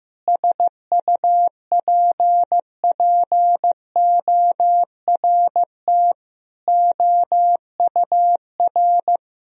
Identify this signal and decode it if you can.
Morse code, heard as a keyed tone, about 15 wpm, 700 Hz. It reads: SUPPORT OUR
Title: Sentences from Top 1000 Words - Review Part 4 - 15wpm